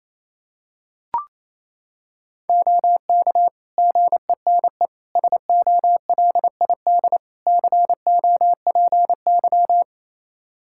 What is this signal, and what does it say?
Morse code, heard as a keyed tone, about 28 wpm, 700 Hz. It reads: OK GENE SOLID COPY
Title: Practice QSO - Exchange 3 - More Info - 28wpm